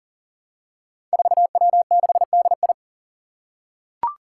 Morse code, heard as a keyed tone, about 39 wpm, 700 Hz. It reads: 4W6DI
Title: Five Challenging Letter-Number Combinations 40wpm